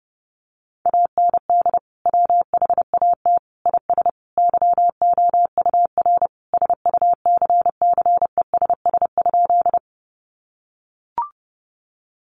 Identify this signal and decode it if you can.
Morse code, heard as a keyed tone, about 30 wpm, 700 Hz. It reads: AND WHAT IS YOUR SUCCESS?
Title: Sentences from Top 900 Words 30wpm